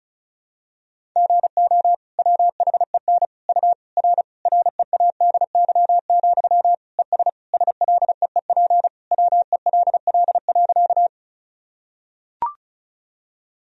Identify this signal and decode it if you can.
Morse code, heard as a keyed tone, about 36 wpm, 700 Hz. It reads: GO WHEN U R READY, ES SLEEP WELL.
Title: Sentences from Top 500 Words and Abbreviations 35wpm